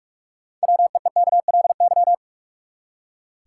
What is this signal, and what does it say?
Morse code, heard as a keyed tone, about 45 wpm, 700 Hz. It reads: WEEKLY
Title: Six Letter Words - 45wpm